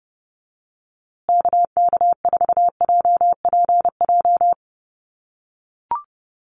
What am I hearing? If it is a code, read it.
Morse code, heard as a keyed tone, about 30 wpm, 700 Hz. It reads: KK4JPJ
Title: USA Call Signs Repeated 2x 30wpm